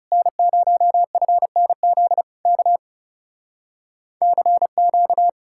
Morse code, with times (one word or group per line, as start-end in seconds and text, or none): 0.1-2.2 N0FNZ
2.5-2.8 K
4.2-5.3 CQ